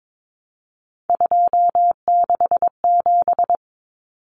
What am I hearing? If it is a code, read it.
Morse code, heard as a keyed tone, about 22 wpm, 700 Hz. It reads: 267
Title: Common Cut Numbers - 3-Digit 22wpm